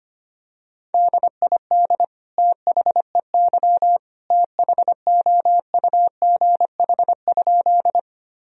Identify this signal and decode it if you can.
Morse code, heard as a keyed tone, about 25 wpm, 700 Hz. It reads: DID THEY THOUGH?
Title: Sentences from Top 1000 Words - Review Part 2 - 25wpm